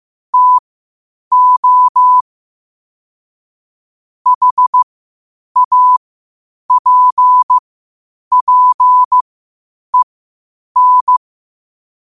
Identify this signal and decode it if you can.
Morse code, heard as a keyed tone, about 15 wpm, spaced as for 8 wpm, 1000 Hz. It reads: TO HAPPEN
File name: qotdfiles / 8wpm